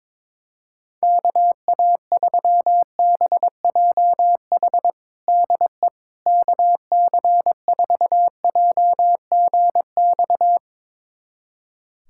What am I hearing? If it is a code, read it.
Morse code, heard as a keyed tone, about 22 wpm, 700 Hz. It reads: KA3BJH DE KC4JGX
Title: USA Callsign DE Callsign - 6 Characters - 22wpm